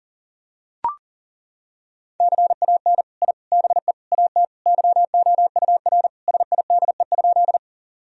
Morse code, heard as a keyed tone, about 40 wpm, 700 Hz. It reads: CAN I BE AT YOUR SIDE?
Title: Sentences from Top 1000 Words - Review Part 1 - 40wpm